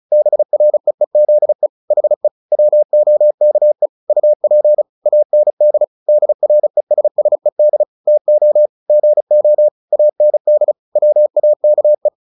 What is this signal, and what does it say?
Morse code, heard as a keyed tone, about 34 wpm, 600 Hz. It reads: BREEZE HE WOKE UP AND DRESSED TO GO AND WAKE
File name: the old man and the sea / the old man and the sea 1 35wpm